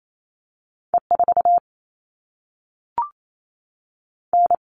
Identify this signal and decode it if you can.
Morse code, heard as a keyed tone, about 28 wpm, 700 Hz. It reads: E4 N
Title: All Cut Numbers - 2-Digit 28wpm